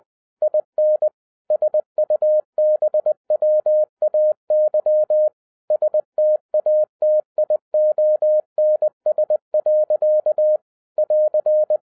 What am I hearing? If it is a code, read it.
Morse code, heard as a keyed tone, about 20 wpm, 610 Hz. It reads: IN SUBWAY STATIONS. <AR>
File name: 2011-07-mp3 / n33 THE 3.11 2 20wpm